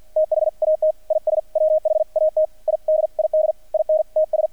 Morse code, heard as a keyed tone, about 48 wpm, 620 Hz. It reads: THAT IS WHAT I DID IN TS
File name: archived / CRSnet